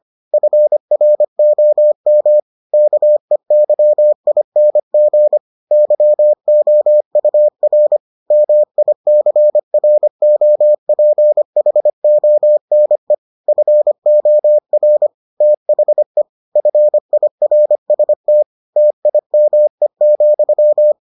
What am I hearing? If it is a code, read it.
Morse code, heard as a keyed tone, about 25 wpm, 600 Hz. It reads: FROM KEYING YOUR MICROPHONE FOR THE FIRST TIME,